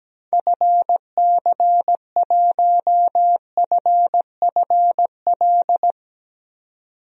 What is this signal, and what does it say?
Morse code, heard as a keyed tone, about 17 wpm, 700 Hz. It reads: FC1FFL
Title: DXCC-100 Call Signs - 6 Characters - 17wpm